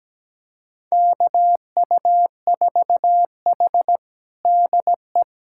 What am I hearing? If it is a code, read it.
Morse code, heard as a keyed tone, about 17 wpm, 700 Hz. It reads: KU4H DE
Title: USA Callsign DE Callsign - 4 Characters - 17wpm